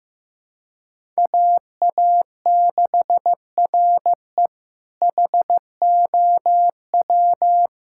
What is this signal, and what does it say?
Morse code, heard as a keyed tone, about 15 wpm, 700 Hz. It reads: AA6RE HOW
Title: K1USN SST QSO Practice - 15wpm